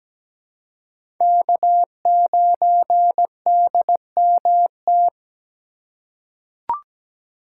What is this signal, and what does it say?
Morse code, heard as a keyed tone, about 17 wpm, 700 Hz. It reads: K9DMT
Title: USA Call Signs - 5 Characters - 3x Character Spacing - 17wpm